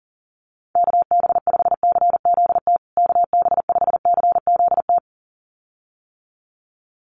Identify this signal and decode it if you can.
Morse code, heard as a keyed tone, about 40 wpm, 700 Hz. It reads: KB5CZT KB5CZT
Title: USA Call Signs Repeated 2x 40wpm